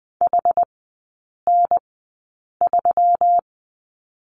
Morse code, heard as a keyed tone, about 20 wpm, spaced as for 8 wpm, 700 Hz. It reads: HN3
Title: 121 Lesson 20e - Number 3 - Three Characters 20wpm